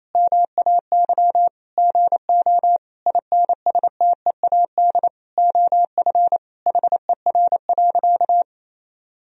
Morse code, heard as a keyed tone, about 28 wpm, 700 Hz. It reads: MAY GO INSTEAD OF HER.
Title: Sentences from Top 1000 Words - Review Part 4 - 28wpm